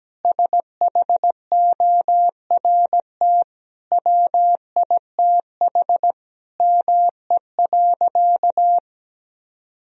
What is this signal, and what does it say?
Morse code, heard as a keyed tone, about 17 wpm, 700 Hz. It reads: SHORT WITH ME.
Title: Sentences from Top 1000 Words - Review Part 2 - 17wpm